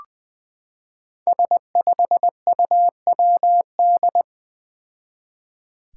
Morse code, heard as a keyed tone, about 20 wpm, 700 Hz. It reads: S5UWD
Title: Five Challenging Letter-Number Combinations 20wpm